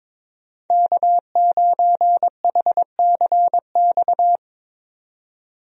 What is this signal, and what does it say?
Morse code, heard as a keyed tone, about 22 wpm, 700 Hz. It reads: K9HCX
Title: USA Call Signs - 5 Characters - 22wpm